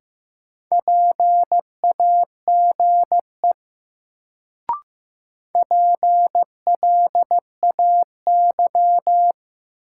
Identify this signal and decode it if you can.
Morse code, heard as a keyed tone, about 15 wpm, 700 Hz. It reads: PAGE PLAY
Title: Sets of 5 Words Encore - New Word Drill - 15wpm